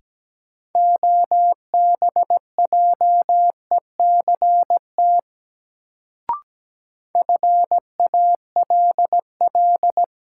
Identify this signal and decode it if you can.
Morse code, heard as a keyed tone, about 17 wpm, 700 Hz. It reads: OBJECT FALL
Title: Sentences from Top 500 Words - New Word Drill - 17wpm